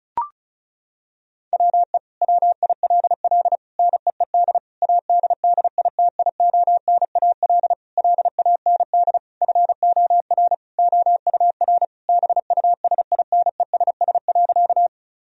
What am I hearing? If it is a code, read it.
Morse code, heard as a keyed tone, about 36 wpm, 700 Hz. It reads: WE WILL NEED ADDITIONAL LAND FOR OUR BUSINESS.
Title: Sentences from Top 1000 Words - Review Part 4 - 35wpm